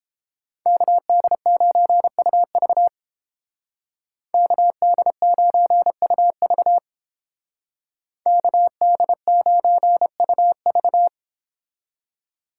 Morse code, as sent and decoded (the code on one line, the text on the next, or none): -.- -.. ----. ..- ...- / -.- -.. ----. ..- ...- / -.- -.. ----. ..- ...-
KD9UV KD9UV KD9UV